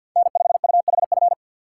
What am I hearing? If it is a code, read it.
Morse code, heard as a keyed tone, about 64 wpm, 700 Hz. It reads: N5UHF